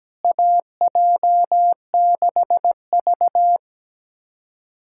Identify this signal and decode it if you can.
Morse code, heard as a keyed tone, about 17 wpm, 700 Hz. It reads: AJ6V
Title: USA Callsign DE Callsign - 4 Characters - 17wpm